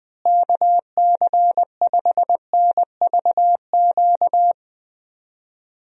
Morse code, heard as a keyed tone, about 20 wpm, 700 Hz. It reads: KC5NVQ
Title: USA Call Signs - 6 Characters - Repeated 2x - 20wpm